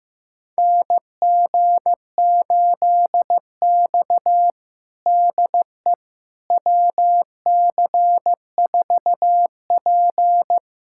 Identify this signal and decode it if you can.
Morse code, heard as a keyed tone, about 15 wpm, 700 Hz. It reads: NG8X DE WC4P
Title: USA Callsign DE Callsign - 4 Characters - 15wpm